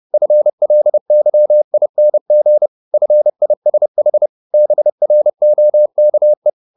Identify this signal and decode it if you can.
Morse code, heard as a keyed tone, about 30 wpm, 600 Hz. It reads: FLYING FISH BROKE
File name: the old man and the sea / the old man and the sea 1 30wpm